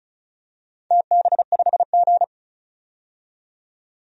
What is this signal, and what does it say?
Morse code, heard as a keyed tone, about 36 wpm, 700 Hz. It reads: TB5G
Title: DXCC-100 Call Signs - 4 Characters - 35wpm